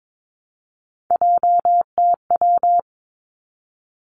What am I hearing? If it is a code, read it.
Morse code, heard as a keyed tone, about 22 wpm, 700 Hz. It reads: JTW
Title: Three Challenging Letter-Number Combinations 22wpm